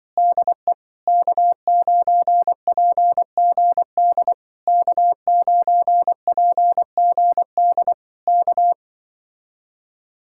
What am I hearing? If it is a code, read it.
Morse code, heard as a keyed tone, about 25 wpm, 700 Hz. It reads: DE K9PGD K9PGD K